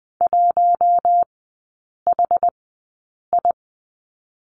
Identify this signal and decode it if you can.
Morse code, heard as a keyed tone, about 20 wpm, spaced as for 8 wpm, 700 Hz. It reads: 1HI
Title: 060 Lesson 11e - Letter h - Three Characters 20wpm